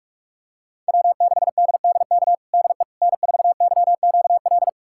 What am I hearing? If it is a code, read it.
Morse code, heard as a keyed tone, about 46 wpm, 700 Hz. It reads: W6DDK DE N4YQL
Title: USA Callsign DE Callsign - 5 Characters - 45wpm